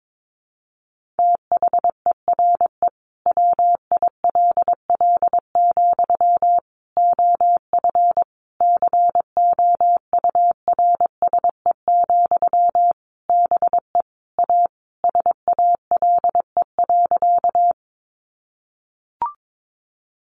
Morse code, heard as a keyed tone, about 22 wpm, 700 Hz. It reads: THERE WILL, OF COURSE, BE A SALE.
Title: Sentences from Top 1400 Words 22wpm